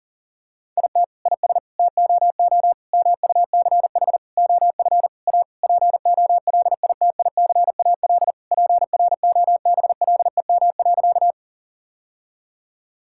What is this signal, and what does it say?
Morse code, heard as a keyed tone, about 40 wpm, 700 Hz. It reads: IT IS TOO MUCH OF A POLITICAL PROBLEM.
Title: Sentences from Top 1000 Words - Review Part 5 - 40wpm